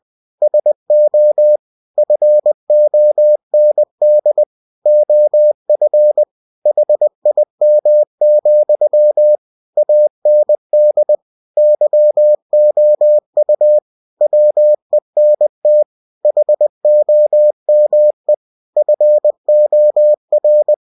Morse code, heard as a keyed tone, about 20 wpm, 600 Hz. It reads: SO FOND OF HIM, AND YOU WENT HOME FOR